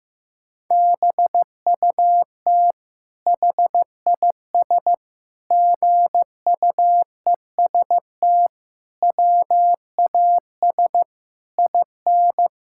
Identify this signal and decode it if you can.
Morse code, heard as a keyed tone, about 15 wpm, 700 Hz. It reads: BUT HIS GUEST WAS IN
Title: Sentences from Top 1600 Words 15wpm